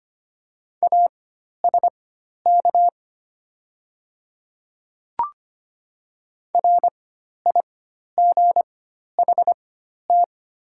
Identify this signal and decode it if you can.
Morse code, heard as a keyed tone, about 25 wpm, spaced as for 12 wpm, 700 Hz. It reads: ASK RIGHT
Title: 4x Character Spacing - Top 300 Words - 25wpm